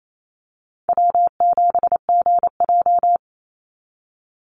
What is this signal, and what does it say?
Morse code, heard as a keyed tone, about 28 wpm, 700 Hz. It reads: W7GJ
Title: NA SOTA Chaser Callsigns 28wpm